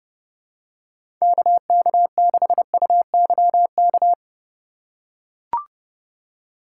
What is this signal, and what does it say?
Morse code, heard as a keyed tone, about 32 wpm, 700 Hz. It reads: KK6UYK
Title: USA Call Signs - 6 Characters - 2x Character Spacing - 30wpm